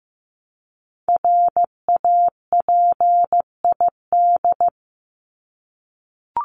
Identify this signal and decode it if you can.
Morse code, heard as a keyed tone, about 15 wpm, 700 Hz. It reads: RAPID
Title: LY Suffix - Top 100 Words with Base Words 15wpm